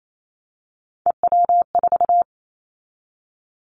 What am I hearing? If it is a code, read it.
Morse code, heard as a keyed tone, about 28 wpm, 700 Hz. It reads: EW4
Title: Callsign Prefixes for Top 50 DXCC - 28wpm